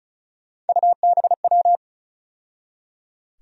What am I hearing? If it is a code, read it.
Morse code, heard as a keyed tone, about 36 wpm, 700 Hz. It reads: UBW
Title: USA Call Sign Postfixes - 3 Letters 35wpm